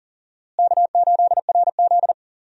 Morse code, heard as a keyed tone, about 39 wpm, 700 Hz. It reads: K8RZ
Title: USA Call Signs - 4 Characters - Repeated 3x - 40wpm